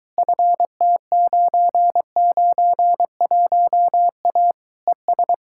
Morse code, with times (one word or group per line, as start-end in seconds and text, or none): 0.2-4.5 FT991A
4.9-5.3 ES